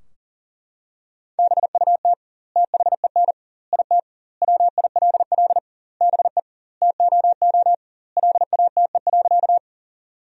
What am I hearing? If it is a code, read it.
Morse code, heard as a keyed tone, about 40 wpm, 700 Hz. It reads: BUT THEN IT WILL BE TOO LATE.
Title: 2x Word-Spacing - Sentences from Top 300 Words - 40wpm